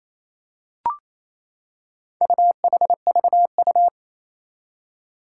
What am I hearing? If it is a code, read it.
Morse code, heard as a keyed tone, about 29 wpm, 700 Hz. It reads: UHVU
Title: Four Challenging Letter-Number Combinations 28wpm